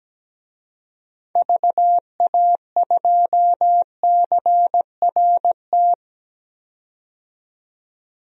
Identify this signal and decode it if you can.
Morse code, heard as a keyed tone, about 17 wpm, 700 Hz. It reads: VA2CRT
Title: DXCC-100 Call Signs - 6 Characters - 17wpm